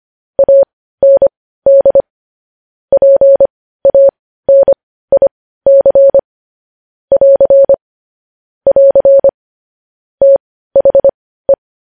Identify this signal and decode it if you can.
Morse code, heard as a keyed tone, about 25 wpm, spaced as for 15 wpm, 550 Hz. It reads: AND PANIC <AR> <AR> THE
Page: news-headlines-in-morse-code-at-15-wpm